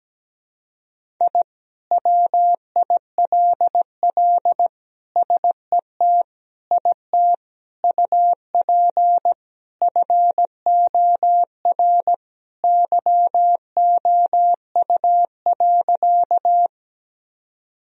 Morse code, heard as a keyed tone, about 17 wpm, 700 Hz. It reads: I WILL SET IT UP FOR YOU.
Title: Sentences from Top 1000 Words - Review Part 1 - 17wpm